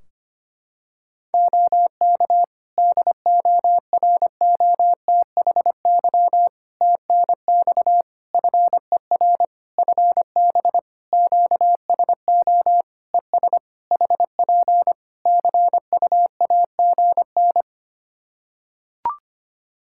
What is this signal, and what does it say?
Morse code, heard as a keyed tone, about 25 wpm, 700 Hz. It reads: OK DOROTHY TNX FER FB QSO ES HP CUAGN
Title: Practice QSO - Exchange 4 - Ending - 25wpm